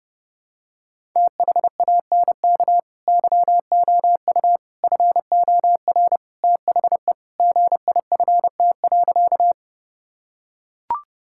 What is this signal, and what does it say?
Morse code, heard as a keyed tone, about 32 wpm, 700 Hz. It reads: THANK YOU FOR THE GIFT.
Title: Sentences from Top 1000 Words - Review Part 6 - 30wpm